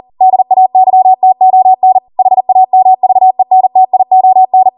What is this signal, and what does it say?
Morse code, heard as a keyed tone, about 40 wpm, 750 Hz. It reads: DAYTON HAMVENTION